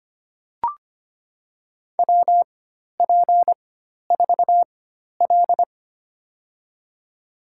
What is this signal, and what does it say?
Morse code, heard as a keyed tone, about 25 wpm, spaced as for 12 wpm, 700 Hz. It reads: WP4L
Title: USA Call Signs - 4x Character Spacing - 25wpm